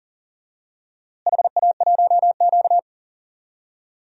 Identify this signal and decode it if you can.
Morse code, heard as a keyed tone, about 40 wpm, 700 Hz. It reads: HA1Q
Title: DXCC-100 Call Signs - 4 Characters - Repeated 3x - 40wpm